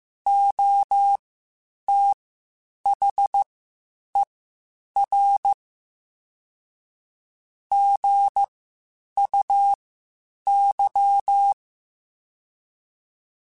Morse code, heard as a keyed tone, about 15 wpm, spaced as for 8 wpm, 800 Hz. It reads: OTHER GUY
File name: Spaced / Chuckles Spaced-mp3s / HumanErr sp @13